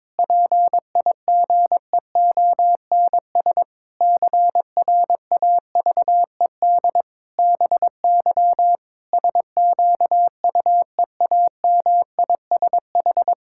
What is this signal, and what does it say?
Morse code, heard as a keyed tone, about 22 wpm, 700 Hz. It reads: PIGEONS CRAVED BY SQUEAMISH
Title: Pangram Challenge v2 - 22wpm